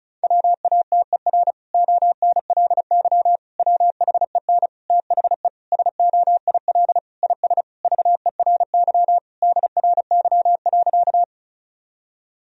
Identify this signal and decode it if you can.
Morse code, heard as a keyed tone, about 36 wpm, 700 Hz. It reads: WATER ONLY WHEN THE SOIL IS VERY DRY.
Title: Sentences from Top 1000 Words - Review Part 4 - 35wpm